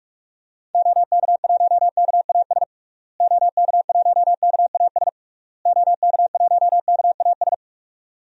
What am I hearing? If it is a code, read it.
Morse code, heard as a keyed tone, about 46 wpm, 700 Hz. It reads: OK1KAS OK1KAS OK1KAS
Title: DXCC-100 Call Signs - 6 Characters - Repeated 3x - 45wpm